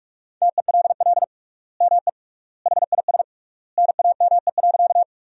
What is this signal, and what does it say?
Morse code, heard as a keyed tone, about 43 wpm, 700 Hz. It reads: TELL ME HIS NAME.
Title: 3x Word-Spacing - Sentences from Top 200 Words - 45wpm